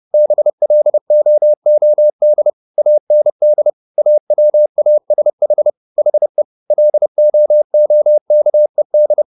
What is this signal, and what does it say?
Morse code, heard as a keyed tone, about 30 wpm, 600 Hz. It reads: BLOOD AND AWASH HE LOOKED